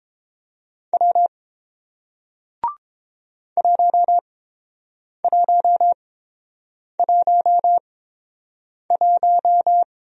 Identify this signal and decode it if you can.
Morse code, heard as a keyed tone, about 30 wpm, spaced as for 7 wpm, 700 Hz. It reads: W 1111